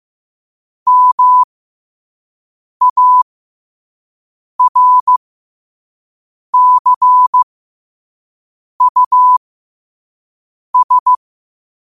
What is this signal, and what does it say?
Morse code, heard as a keyed tone, about 15 wpm, spaced as for 5 wpm, 1000 Hz. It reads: MARCUS
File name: qotdfiles / 5wpm